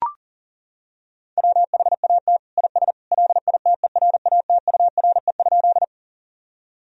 Morse code, heard as a keyed tone, about 40 wpm, 700 Hz. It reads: WHAT IS LITERATURE?
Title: Sentences from Top 1000 Words - Review Part 3 - 40wpm